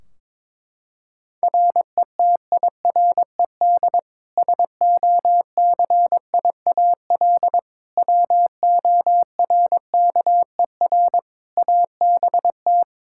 Morse code, heard as a keyed tone, about 22 wpm, 700 Hz. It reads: RETIRED SOCIAL WORKER ABT
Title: K1USN SST QSO Practice 22wpm